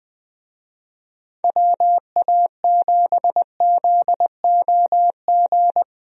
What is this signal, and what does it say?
Morse code, heard as a keyed tone, about 20 wpm, 700 Hz. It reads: WA7ZOG
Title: USA Call Signs - 6 Characters - 20wpm